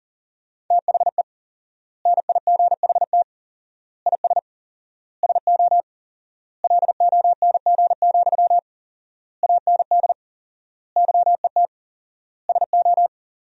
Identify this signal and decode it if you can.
Morse code, heard as a keyed tone, about 40 wpm, 700 Hz. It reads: THE NIGHT IS SO LONG, AND YET SO
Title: 4x Word-Spacing - Sentences from Top 500 Words - 40wpm